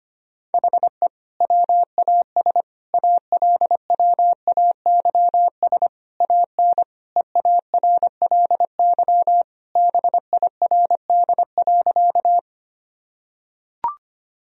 Morse code, heard as a keyed tone, about 25 wpm, 700 Hz. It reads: HE WAS ALWAYS AN EARLY BIRD.
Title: Sentences from Top 1000 Words - Review Part 2 - 25wpm